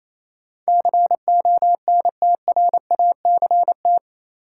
Morse code, heard as a keyed tone, about 28 wpm, 700 Hz. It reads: CONTRACT
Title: Sentences from Top 1400 Words - New Word Drill - 28wpm